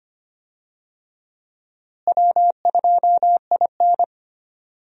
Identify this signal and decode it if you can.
Morse code, heard as a keyed tone, about 25 wpm, 700 Hz. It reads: W2IN
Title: USA Call Signs - 4 Characters - Repeated 2x - 25wpm